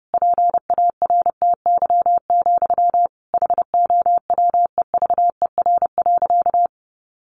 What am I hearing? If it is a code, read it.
Morse code, heard as a keyed tone, about 30 wpm, 700 Hz. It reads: PARTY, HOWEVER.
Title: Sentences from Top 1400 Words 30wpm